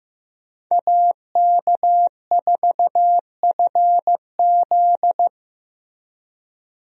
Morse code, heard as a keyed tone, about 14 wpm, 700 Hz. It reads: AK4FZ
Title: USA Call Signs - 5 Characters - Repeated 2x - 15wpm